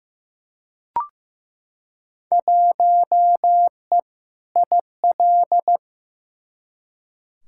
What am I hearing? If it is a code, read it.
Morse code, heard as a keyed tone, about 15 wpm, 700 Hz. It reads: 1E IL
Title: Field Day Exchanges - 15wpm